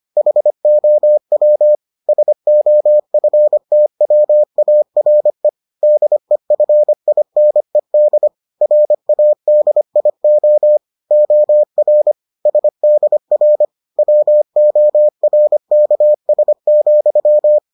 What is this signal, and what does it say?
Morse code, heard as a keyed tone, about 25 wpm, 600 Hz. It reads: HOW SOFTWARE DEFINED RADIO OR SDR WORKS,